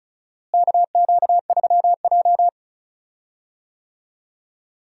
Morse code, heard as a keyed tone, about 36 wpm, 700 Hz. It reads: KQ3J
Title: USA Call Signs - 4 Characters - Repeated 3x - 35wpm